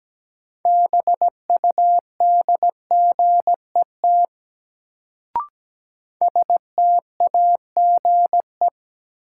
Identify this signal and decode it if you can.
Morse code, heard as a keyed tone, about 17 wpm, 700 Hz. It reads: BUDGET STAGE
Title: Sentences from Top 1200 Words - New Word Drill - 17wpm